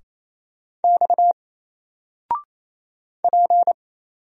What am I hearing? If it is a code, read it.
Morse code, heard as a keyed tone, about 28 wpm, 700 Hz. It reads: X P
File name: Single-Letter-Number-28wpm